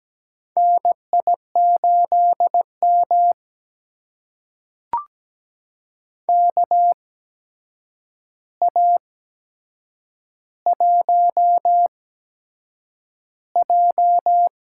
Morse code, heard as a keyed tone, about 17 wpm, spaced as for 7 wpm, 700 Hz. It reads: NI8M KA1J
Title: USA Call Signs - 8x Character Spacing - 17wpm